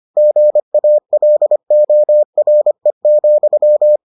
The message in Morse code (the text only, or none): GALORE,